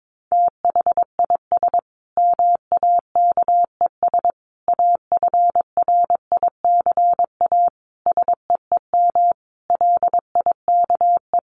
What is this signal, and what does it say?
Morse code, heard as a keyed tone, about 22 wpm, 700 Hz. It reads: THIS MAKES AFRICA SEEM LIKE
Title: A Sound Of Thunder 22wpm